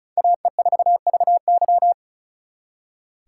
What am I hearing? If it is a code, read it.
Morse code, heard as a keyed tone, about 36 wpm, 700 Hz. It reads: AE4VY
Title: Speed Racing - USA 2x2 Call Signs 35wpm